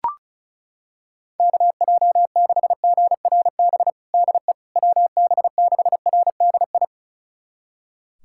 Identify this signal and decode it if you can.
Morse code, heard as a keyed tone, about 36 wpm, 700 Hz. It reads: KJ6GRB DE WB6RDI
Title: USA Callsign DE Callsign - 6 Characters - 35wpm